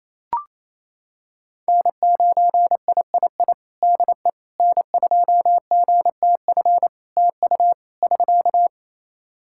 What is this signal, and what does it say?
Morse code, heard as a keyed tone, about 28 wpm, 700 Hz. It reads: N9III DE N2GTF TU <SK>
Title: Practice QSO - Exchange 4 - Ending - 28wpm